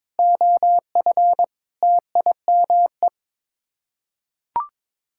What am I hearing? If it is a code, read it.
Morse code, heard as a keyed tone, about 22 wpm, 700 Hz. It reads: OF TIME
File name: practice / 22wpm / Build A Phrase - 5-Word Phrases - 22wpm